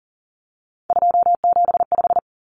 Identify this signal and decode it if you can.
Morse code, heard as a keyed tone, about 40 wpm, 700 Hz. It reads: 275
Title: Numbers - 3-Digit 40wpm